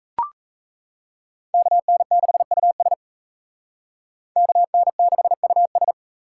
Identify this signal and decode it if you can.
Morse code, heard as a keyed tone, about 36 wpm, 700 Hz. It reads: KN6US KN6US